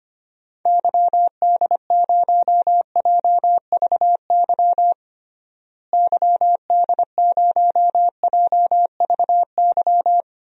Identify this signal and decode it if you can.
Morse code, heard as a keyed tone, about 25 wpm, 700 Hz. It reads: YD0JVY YD0JVY
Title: DXCC-100 Call Signs - 6 Characters - Repeated 2x - 25wpm